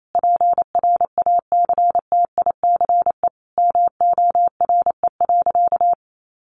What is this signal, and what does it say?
Morse code, heard as a keyed tone, about 28 wpm, 700 Hz. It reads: PRACTICE MORE.
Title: Sentences from Top 700 Words 28wpm